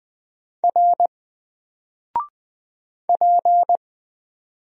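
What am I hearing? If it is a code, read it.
Morse code, heard as a keyed tone, about 20 wpm, 700 Hz. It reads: R P
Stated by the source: Single letter and number Morse drill at 20 wpm, from N4LCD.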